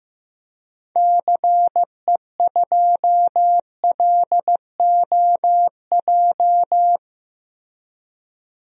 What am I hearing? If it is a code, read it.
Morse code, heard as a keyed tone, about 15 wpm, 700 Hz. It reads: CE2LOJ
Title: DXCC-100 Call Signs - 6 Characters - Repeated 3x - 15wpm